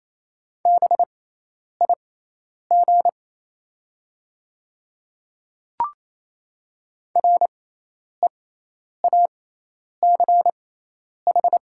Morse code, heard as a keyed tone, about 28 wpm, spaced as for 10 wpm, 700 Hz. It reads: BIG REACH
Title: 6x Character Spacing - Top 500 Words - 28wpm